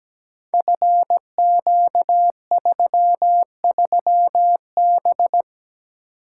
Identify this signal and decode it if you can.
Morse code, heard as a keyed tone, about 17 wpm, 700 Hz. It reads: FQ33B
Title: Code Groups - 5 Letter:Numbers - 17wpm